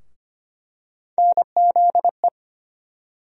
Morse code, heard as a keyed tone, about 25 wpm, 700 Hz. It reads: NZE
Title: USA Call Sign Postfixes - 3 Letters 25wpm